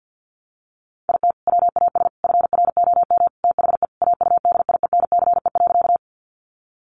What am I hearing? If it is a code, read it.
Morse code, heard as a keyed tone, about 50 wpm, 700 Hz. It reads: IT WAS FROM THE AUDIENCE.
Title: Sentences from Top 800 Words 50wpm